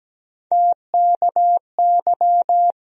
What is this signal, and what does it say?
Morse code, heard as a keyed tone, about 17 wpm, 700 Hz. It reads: TKY
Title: USA Call Sign Postfixes - 3 Letters 17wpm